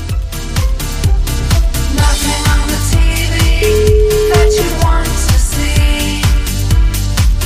Ringback tone (Europe, 425 Hz), with noise.